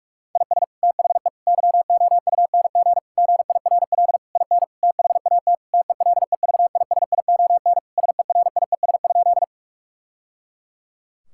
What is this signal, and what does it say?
Morse code, heard as a keyed tone, about 45 wpm, 700 Hz. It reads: IS THE YOUNG GIRL IN THAT TELEVISION SERIES?
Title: Sentences from Top 1000 Words - Review Part 5 - 45wpm